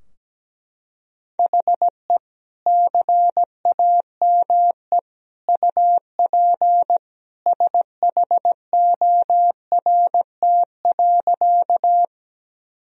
Morse code, heard as a keyed tone, about 17 wpm, 700 Hz. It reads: HE CAME UP SHORT.
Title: Sentences from Top 1000 Words - Review Part 2 - 17wpm